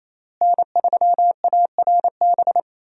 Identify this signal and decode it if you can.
Morse code, heard as a keyed tone, about 28 wpm, 700 Hz. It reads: N3ARB
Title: USA Call Signs - 5 Characters - Repeated 2x - 28wpm